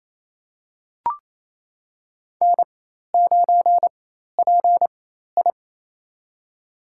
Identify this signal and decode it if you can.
Morse code, heard as a keyed tone, about 27 wpm, spaced as for 13 wpm, 700 Hz. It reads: N9PI
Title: USA Call Signs - 4 Characters - 4x Character Spacing - 28wpm